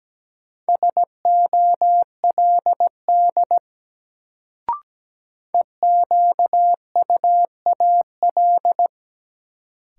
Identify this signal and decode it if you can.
Morse code, heard as a keyed tone, about 17 wpm, 700 Hz. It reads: SOLD EQUAL
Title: Sentences from Top 1400 Words - New Word Drill Alt - 17wpm